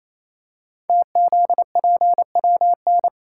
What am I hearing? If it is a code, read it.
Morse code, heard as a keyed tone, about 28 wpm, 700 Hz. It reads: TZPWN
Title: Code Groups - 5 Letter:Numbers - 28wpm